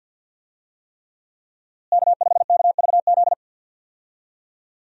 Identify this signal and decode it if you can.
Morse code, heard as a keyed tone, about 47 wpm, 700 Hz. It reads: K5KVZ
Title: USA Call Signs - 5 Characters - Repeated 3x - 50wpm